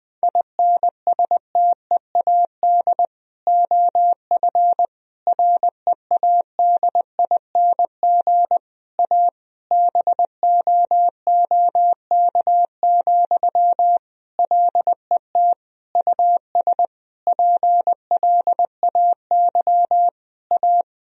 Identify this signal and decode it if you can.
Morse code, heard as a keyed tone, about 20 wpm, 700 Hz. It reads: INSTEAD OF READING A BOOK, LET US PLAY A
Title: Sentences from Top 1000 Words - Review Part 4 - 20wpm